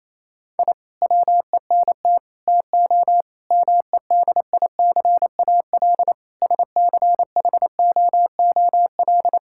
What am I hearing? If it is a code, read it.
Morse code, heard as a keyed tone, about 28 wpm, 700 Hz. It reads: I WENT TO MEDICAL SCHOOL
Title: Sentences from Top 1000 Words - Review Part 5 - 28wpm